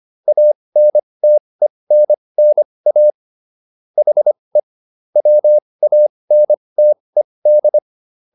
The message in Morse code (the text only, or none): ANTENNA HE WANTED